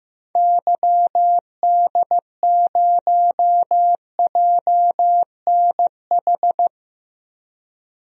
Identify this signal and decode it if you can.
Morse code, heard as a keyed tone, about 15 wpm, 700 Hz. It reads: YD0JNH
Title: DXCC-100 Call Signs - 6 Characters - 15wpm